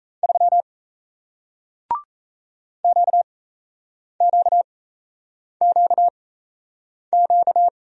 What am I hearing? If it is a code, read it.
Morse code, heard as a keyed tone, about 38 wpm, spaced as for 8 wpm, 700 Hz. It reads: 3 QQQQ